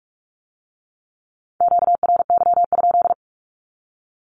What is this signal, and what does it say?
Morse code, heard as a keyed tone, about 46 wpm, 700 Hz. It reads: QRY?
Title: Q-Codes with Spoken Meaning - 45wpm